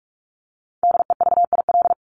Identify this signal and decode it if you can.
Morse code, heard as a keyed tone, about 44 wpm, 700 Hz. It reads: DEVIL
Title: Top 1751-2000 Words 45wpm